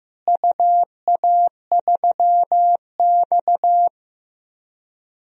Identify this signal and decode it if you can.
Morse code, heard as a keyed tone, about 15 wpm, 700 Hz. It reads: UA3X
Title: DXCC-100 Call Signs - 4 Characters - Repeated 2x - 15wpm